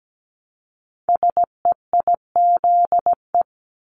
Morse code, heard as a keyed tone, about 17 wpm, 700 Hz. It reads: SEIZE
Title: Multi-letter Phonemes Repeated 3x 17wpm